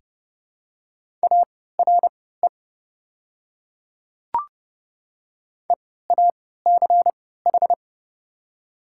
Morse code, heard as a keyed tone, about 30 wpm, spaced as for 17 wpm, 700 Hz. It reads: ARE EACH
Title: 3x Character Spacing - Top 100 Words - 30wpm